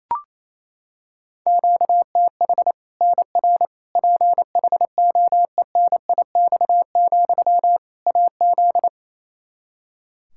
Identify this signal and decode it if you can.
Morse code, heard as a keyed tone, about 28 wpm, 700 Hz. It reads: QTH NR PHOENIX, AZ
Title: Practice QSO - Exchange 1 - Beginning - 28wpm